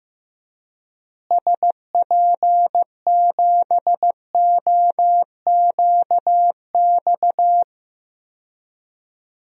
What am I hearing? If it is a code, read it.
Morse code, heard as a keyed tone, about 15 wpm, 700 Hz. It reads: SP7OQX
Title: DXCC-100 Call Signs - 6 Characters - Repeated 2x - 15wpm